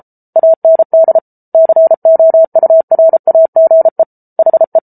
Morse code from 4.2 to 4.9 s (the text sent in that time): HE